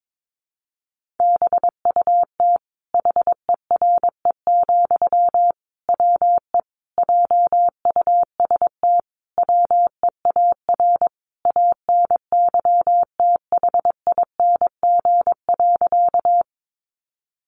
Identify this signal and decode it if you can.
Morse code, heard as a keyed tone, about 22 wpm, 700 Hz. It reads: BUT HERE, WE JUST WEAR ANYTHING.
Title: Sentences from Top 1500 Words 22wpm